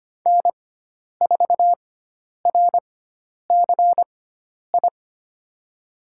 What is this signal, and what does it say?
Morse code, heard as a keyed tone, about 25 wpm, spaced as for 10 wpm, 700 Hz. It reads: N4RCI
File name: Lesson 16g - Letter c - Call Signs 25wpm